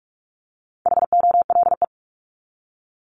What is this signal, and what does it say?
Morse code, heard as a keyed tone, about 44 wpm, 700 Hz. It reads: HOLE
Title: Top 1501-1750 Words 45wpm